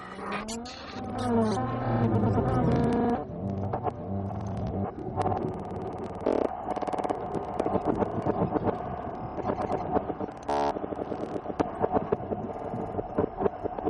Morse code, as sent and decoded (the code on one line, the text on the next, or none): none